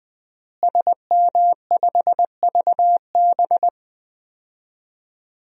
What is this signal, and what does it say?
Morse code, heard as a keyed tone, about 20 wpm, 700 Hz. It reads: SM5VB
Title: DXCC-100 Call Signs - 5 Characters - Repeated 3x - 20wpm